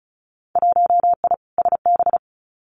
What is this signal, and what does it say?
Morse code, heard as a keyed tone, about 34 wpm, 700 Hz. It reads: 1I SB
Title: Winter Field Day Exchanges 35wpm